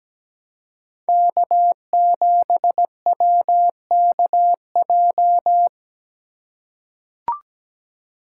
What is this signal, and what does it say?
Morse code, heard as a keyed tone, about 17 wpm, 700 Hz. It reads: K7WKJ
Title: USA Call Signs - 5 Characters - 3x Character Spacing - 17wpm